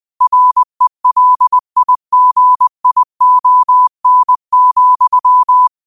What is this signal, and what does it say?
Morse code, heard as a keyed tone, about 20 wpm, 1000 Hz. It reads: RELIGION,